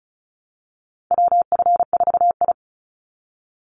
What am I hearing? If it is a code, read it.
Morse code, heard as a keyed tone, about 36 wpm, 700 Hz. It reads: WF4I
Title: NA SOTA Chaser Callsigns 35wpm